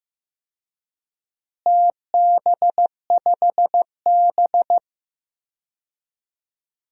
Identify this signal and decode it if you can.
Morse code, heard as a keyed tone, about 15 wpm, 700 Hz. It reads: TB5B
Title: DXCC-100 Call Signs - 4 Characters - Repeated 2x - 15wpm